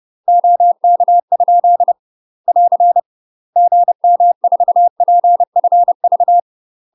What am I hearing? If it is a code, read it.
Morse code, heard as a keyed tone, about 30 wpm, 700 Hz. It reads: OK? <AR> GM4PFV
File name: UK MASTER QSOs TRAINING / 30WPM / UK MASTER QSOs 0042